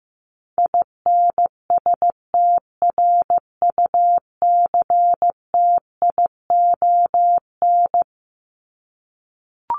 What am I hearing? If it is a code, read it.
Morse code, heard as a keyed tone, about 15 wpm, 700 Hz. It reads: INSTRUCTION
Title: Top 1751-2000 Words 15wpm